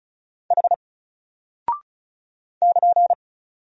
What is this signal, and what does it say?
Morse code, heard as a keyed tone, about 36 wpm, 700 Hz. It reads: H <KN>
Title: Single Character (Letter:Number:Punctuation:Prosign) Alt 35wpm